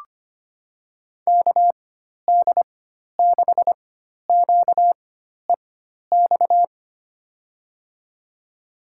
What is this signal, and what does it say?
Morse code, heard as a keyed tone, about 25 wpm, spaced as for 12 wpm, 700 Hz. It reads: KD6QEX
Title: USA Call Signs - 6 Characters - 4x Character Spacing - 25wpm